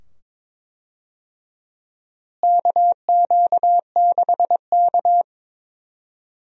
Morse code, heard as a keyed tone, about 22 wpm, 700 Hz. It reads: KQ6K
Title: USA Call Signs - 4 Characters - Repeated 2x - 22wpm